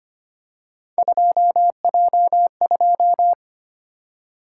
Morse code, heard as a keyed tone, about 25 wpm, 700 Hz. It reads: 2J2
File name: Three Letter-Number Combinations - 25wpm